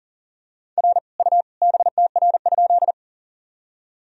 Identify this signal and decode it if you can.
Morse code, heard as a keyed tone, about 40 wpm, 700 Hz. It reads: R U BTR?
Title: Sentences from Top 400 Words and Abbreviations 40wpm